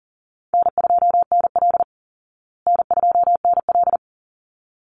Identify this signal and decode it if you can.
Morse code, heard as a keyed tone, about 38 wpm, 700 Hz. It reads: N2NL N2NL
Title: Call Signs Repeated 3x - 40wpm